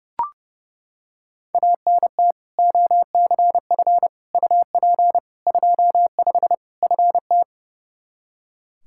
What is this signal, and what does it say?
Morse code, heard as a keyed tone, about 30 wpm, 700 Hz. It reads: ANT OCF UP 25 FT
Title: Practice QSO - Exchange 2 - Info - 30wpm